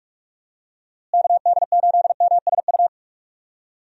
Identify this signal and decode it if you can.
Morse code, heard as a keyed tone, about 47 wpm, 700 Hz. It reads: KD8MSU
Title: USA Call Signs - 6 Characters - Repeated 3x - 45wpm